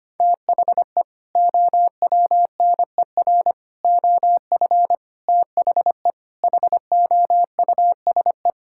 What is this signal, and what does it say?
Morse code, heard as a keyed tone, about 25 wpm, 700 Hz. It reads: THE OWNER OF THE HOUSE
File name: practice / 25wpm / Sentences with 3-Word Phrases - 25wpm